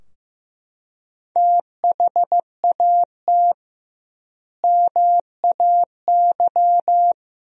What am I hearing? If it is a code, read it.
Morse code, heard as a keyed tone, about 15 wpm, 700 Hz. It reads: THAT MAY
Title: 2x Word-Spacing - Sentences from Top 400 Words - 15wpm